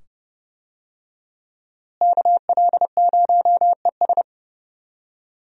Morse code, heard as a keyed tone, about 30 wpm, 700 Hz. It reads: KL0ES
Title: USA Call Signs - 5 Characters - Repeated 2x - 30wpm